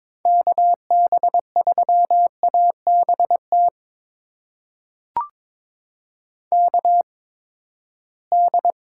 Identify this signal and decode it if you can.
Morse code, heard as a keyed tone, about 23 wpm, spaced as for 15 wpm, 700 Hz. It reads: KB3ABT KD
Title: USA Call Signs - 6 Characters - 8x Character Spacing - 22wpm